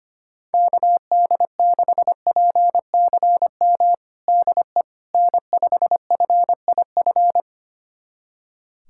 Morse code, heard as a keyed tone, about 25 wpm, 700 Hz. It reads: KD6PCM DE N5FIF
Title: USA Callsign DE Callsign 25wpm